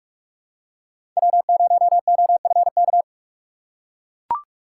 Morse code, heard as a keyed tone, about 43 wpm, 700 Hz. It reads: W0OFK
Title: USA Call Signs - 5 Characters - 4x Character Spacing - 45wpm